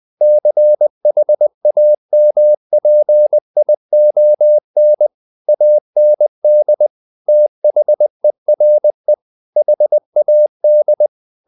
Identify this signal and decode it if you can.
Morse code, heard as a keyed tone, about 20 wpm, 600 Hz. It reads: CHAMPION AND THERE HAD
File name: the old man and the sea part 2 20wpm